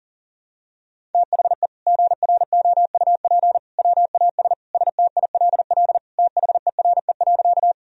Morse code, heard as a keyed tone, about 40 wpm, 700 Hz. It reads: THE GROUP WAS STILL THERE.
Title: Sentences from Top 1000 Words - Review Part 1 - 40wpm